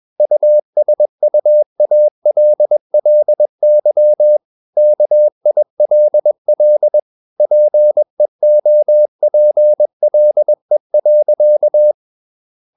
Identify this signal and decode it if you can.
Morse code, heard as a keyed tone, about 21 wpm, 600 Hz. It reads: USUALLY KILL PEOPLE.
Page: CW med Gnister